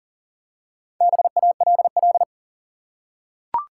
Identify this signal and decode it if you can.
Morse code, heard as a keyed tone, about 40 wpm, 700 Hz. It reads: BALL
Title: Four Letter Words - 40wpm